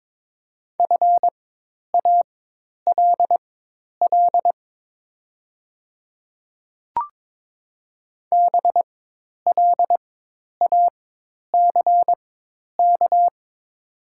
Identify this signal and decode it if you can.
Morse code, heard as a keyed tone, about 22 wpm, spaced as for 10 wpm, 700 Hz. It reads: FALL BLACK
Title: 4x Character Spacing - Top 500 Words - 22wpm